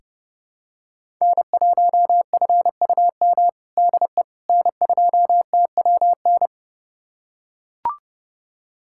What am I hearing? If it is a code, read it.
Morse code, heard as a keyed tone, about 30 wpm, 700 Hz. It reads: N1FUM DE N2TWN
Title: USA Callsign DE Callsign - 5 Characters - 30wpm